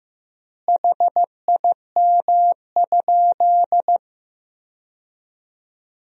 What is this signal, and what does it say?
Morse code, heard as a keyed tone, about 16 wpm, 700 Hz. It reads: HIM?
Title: 3x Word-Spacing - Sentences from Top 400 Words - 15wpm